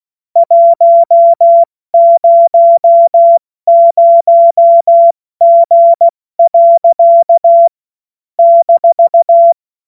Morse code, heard as a keyed tone, about 16 wpm, 680 Hz. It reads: 100G. -
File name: Morsekurs / Witze / Witz 07. Juni 2022 70 F80